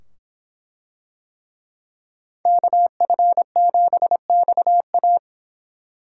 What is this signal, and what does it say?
Morse code, heard as a keyed tone, about 24 wpm, 700 Hz. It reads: KF7XA